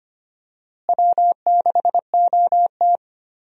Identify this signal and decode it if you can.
Morse code, heard as a keyed tone, about 25 wpm, 700 Hz. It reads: W6OT
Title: USA Call Signs - 4 Characters - Repeated 3x - 25wpm